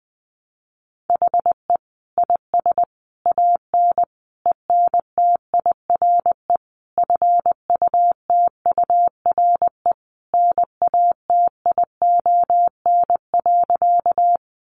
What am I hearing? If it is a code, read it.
Morse code, heard as a keyed tone, about 19 wpm, 700 Hz. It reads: HE IS AN ENTIRE FUTURE NATION.
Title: A Sound Of Thunder 20wpm